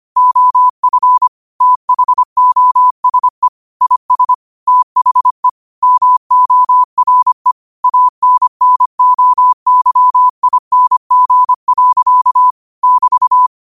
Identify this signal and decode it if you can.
Morse code, heard as a keyed tone, about 25 wpm, 1000 Hz. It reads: OF THOSE IS THE MORE ANNOYING. <BT>
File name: qotdfiles / 25wpm